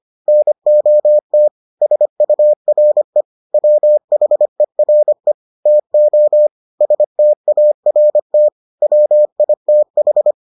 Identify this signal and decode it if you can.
Morse code, heard as a keyed tone, about 25 wpm, 600 Hz. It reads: NOT SURE WHERE TO START WITH